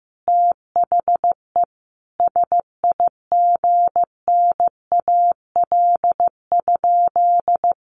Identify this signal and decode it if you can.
Morse code, heard as a keyed tone, about 15 wpm, 700 Hz. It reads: THE SIGNAL?
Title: Sentences from Top 1400 Words 15wpm